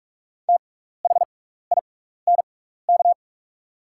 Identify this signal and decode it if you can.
Morse code, heard as a keyed tone, about 44 wpm, spaced as for 15 wpm, 690 Hz. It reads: THINK
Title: 6x Character Spacing - Top 200 Words - 45wpm